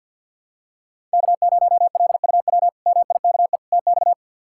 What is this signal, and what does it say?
Morse code, heard as a keyed tone, about 48 wpm, 700 Hz. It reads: K0LUW MIKE TX
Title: K1USN SST QSO Practice - 50wpm